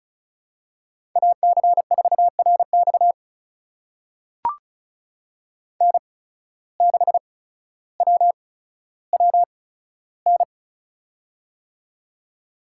Morse code, heard as a keyed tone, about 34 wpm, spaced as for 15 wpm, 700 Hz. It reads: AC4RX N6WWN
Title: USA Call Signs - 5 Characters - 8x Character Spacing - 35wpm